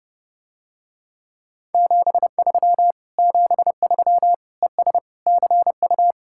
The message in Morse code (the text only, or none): 73 73 ES CU